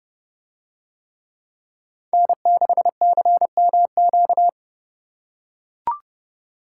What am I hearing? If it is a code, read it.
Morse code, heard as a keyed tone, about 30 wpm, 700 Hz. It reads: N6CMQ